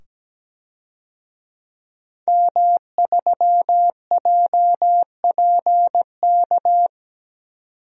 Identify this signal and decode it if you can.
Morse code, heard as a keyed tone, about 17 wpm, 700 Hz. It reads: M3JPK
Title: DXCC-100 Call Signs - 5 Characters - 17wpm